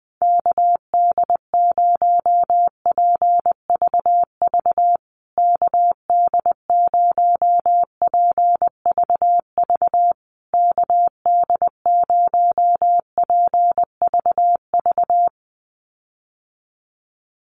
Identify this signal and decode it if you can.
Morse code, heard as a keyed tone, about 20 wpm, 700 Hz. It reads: KD0PVV KD0PVV KD0PVV
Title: USA Call Signs Repeated 3x 20wpm